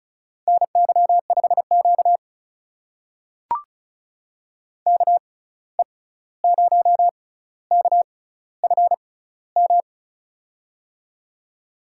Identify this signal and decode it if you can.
Morse code, heard as a keyed tone, about 36 wpm, spaced as for 16 wpm, 700 Hz. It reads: NY5Q KE0KFM
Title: USA Call Signs - 6x Character Spacing - 35wpm